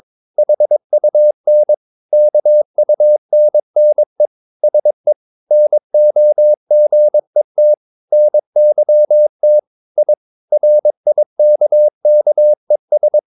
Morse code, heard as a keyed tone, about 22 wpm, 600 Hz. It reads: HUN KUNNE SE NOGET NYT I RIKKES
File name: Ferie QRS 22wpm